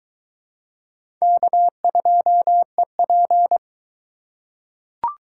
Morse code, heard as a keyed tone, about 21 wpm, 700 Hz. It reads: K2EP